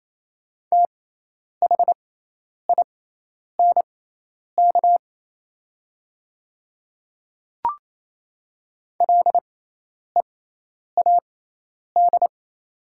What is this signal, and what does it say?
Morse code, heard as a keyed tone, about 28 wpm, spaced as for 10 wpm, 700 Hz. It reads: THINK LEAD
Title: 6x Character Spacing - Top 500 Words - 28wpm